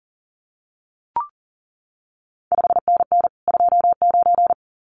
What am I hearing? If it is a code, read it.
Morse code, heard as a keyed tone, about 40 wpm, 700 Hz. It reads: 5NN 29
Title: CQ WW DX Contest Exchange Encore - 40wpm